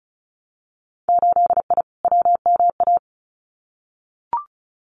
Morse code, heard as a keyed tone, about 36 wpm, 700 Hz. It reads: 8I WMA
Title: Winter Field Day Exchanges Encore 35wpm